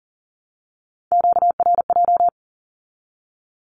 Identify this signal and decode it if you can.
Morse code, heard as a keyed tone, about 38 wpm, 700 Hz. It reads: QRJ
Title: Q-Codes with Spoken Meaning - 40wpm